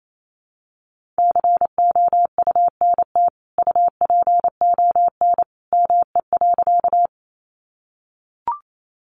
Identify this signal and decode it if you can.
Morse code, heard as a keyed tone, about 28 wpm, 700 Hz. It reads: COUNT UPON ME.
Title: Sentences from Top 1500 Words 28wpm